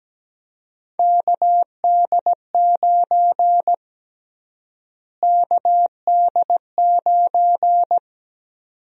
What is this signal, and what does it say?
Morse code, heard as a keyed tone, about 17 wpm, 700 Hz. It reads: KD9 KD9
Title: USA Call Sign Prefixes - Repeated 2x - 17wpm